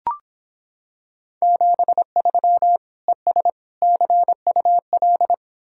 Morse code, heard as a keyed tone, about 26 wpm, 700 Hz. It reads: 73 ES CUL